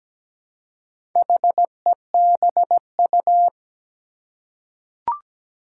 Morse code, heard as a keyed tone, about 17 wpm, 700 Hz. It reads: HEBU
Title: Four Challenging Letter-Number Combinations 17wpm